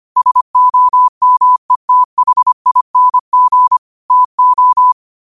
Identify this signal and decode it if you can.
Morse code, heard as a keyed tone, about 25 wpm, 1000 Hz. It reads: SOMETHING TO